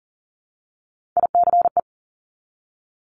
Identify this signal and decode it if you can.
Morse code, heard as a keyed tone, about 38 wpm, 690 Hz. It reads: ICE
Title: Top Three Letter Combinations in Words 40wpm